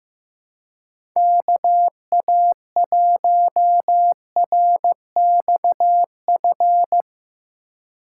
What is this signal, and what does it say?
Morse code, heard as a keyed tone, about 15 wpm, 700 Hz. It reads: KA1RXF
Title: USA Call Signs - 4x Character Spacing - 15wpm